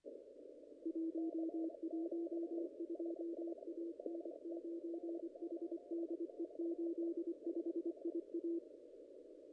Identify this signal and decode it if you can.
Morse code, heard as a keyed tone, about 25 wpm, 340 Hz. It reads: 112 A N9HDE 85 IA
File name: contests / ss-cw-k5tr-op-n5rz -0100